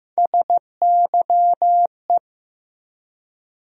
Morse code, heard as a keyed tone, about 15 wpm, 700 Hz. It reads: SYE
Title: USA Call Sign Postfixes - 3 Letters 15wpm